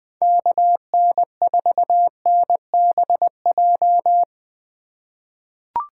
Morse code, heard as a keyed tone, about 20 wpm, 700 Hz. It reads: KN4NBJ
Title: 2023 POTA Activator Call Signs - 20wpm